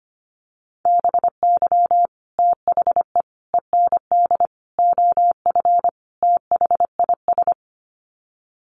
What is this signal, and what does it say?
Morse code, heard as a keyed tone, about 25 wpm, 700 Hz. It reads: BY THE END OF THIS
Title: Sets of 5 Words Encore 25wpm